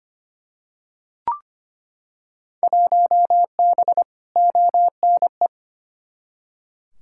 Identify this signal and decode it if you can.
Morse code, heard as a keyed tone, about 25 wpm, 700 Hz. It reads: 1B ONE
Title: Field Day Exchanges - 25wpm